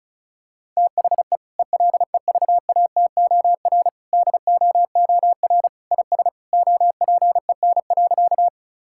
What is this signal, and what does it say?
Morse code, heard as a keyed tone, about 36 wpm, 700 Hz. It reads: THE ELEVATOR DOOR IS OPEN.
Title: Sentences from Top 1600 Words 35wpm